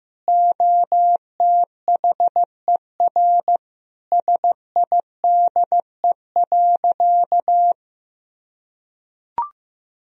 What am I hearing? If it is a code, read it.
Morse code, heard as a keyed tone, about 15 wpm, 700 Hz. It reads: OTHER SIDE.
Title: Sentences from Top 1000 Words - Review Part 1 - 15wpm